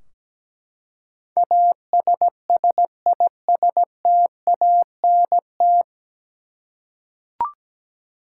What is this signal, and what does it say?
Morse code, heard as a keyed tone, about 17 wpm, 700 Hz. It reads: ASSISTANT
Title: Practice QSO - Exchange 3 - More Info - 17wpm